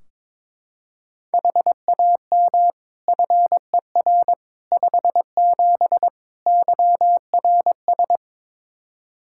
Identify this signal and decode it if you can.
Morse code, heard as a keyed tone, about 22 wpm, 700 Hz. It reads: HAM FER 57 YRS
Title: Practice QSO - Exchange 3 - More Info - 22wpm